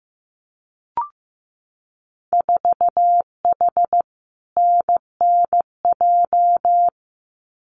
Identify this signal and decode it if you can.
Morse code, heard as a keyed tone, about 15 wpm, 700 Hz. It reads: 4H NNJ
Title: Winter Field Day Exchanges Encore 15wpm